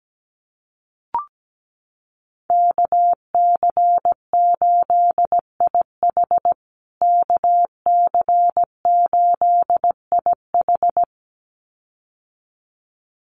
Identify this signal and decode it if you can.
Morse code, heard as a keyed tone, about 17 wpm, 700 Hz. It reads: KC8IH KC8IH
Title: USA Call Signs Repeated 2x 17wpm